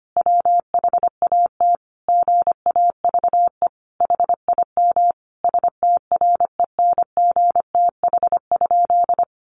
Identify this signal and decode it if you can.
Morse code, heard as a keyed tone, about 25 wpm, 700 Hz. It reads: WHAT GAVE HIM STRENGTH?
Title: Sentences from Top 1300 Words 25wpm